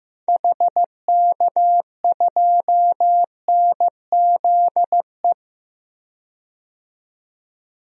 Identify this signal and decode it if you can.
Morse code, heard as a keyed tone, about 15 wpm, 700 Hz. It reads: HK2NZE
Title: DXCC-100 Call Signs - 6 Characters - 15wpm